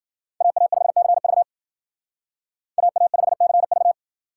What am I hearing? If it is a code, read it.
Morse code, heard as a keyed tone, about 52 wpm, 700 Hz. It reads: AA5CV AA5CV